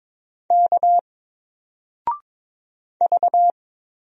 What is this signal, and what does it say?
Morse code, heard as a keyed tone, about 22 wpm, 700 Hz. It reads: K V
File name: Single-Letter-Number-22wpm